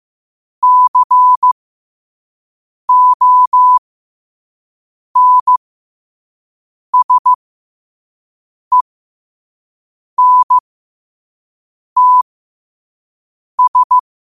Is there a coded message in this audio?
Yes: Morse code from 0.6 to 14.0 s, heard as a keyed tone, about 15 wpm, spaced as for 5 wpm, 1000 Hz. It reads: CONSENTS